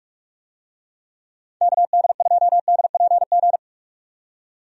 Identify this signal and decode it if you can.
Morse code, heard as a keyed tone, about 41 wpm, 700 Hz. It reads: KD2DPG
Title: USA Call Signs - 6 Characters - 45wpm